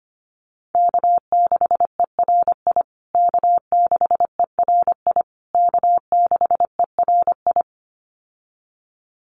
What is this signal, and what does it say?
Morse code, heard as a keyed tone, about 25 wpm, 700 Hz. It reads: K6ERI K6ERI K6ERI
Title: USA Call Signs Repeated 3x 25wpm